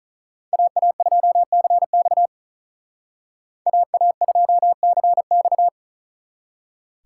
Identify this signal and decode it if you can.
Morse code, heard as a keyed tone, about 42 wpm, 700 Hz. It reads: AA2CX AA2CX
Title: Speed Racing - USA 2x2 Call Signs 35wpm